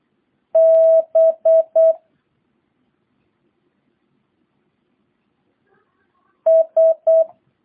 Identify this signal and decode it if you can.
Morse code, heard as a keyed tone, about 8 wpm, 650 Hz. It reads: B S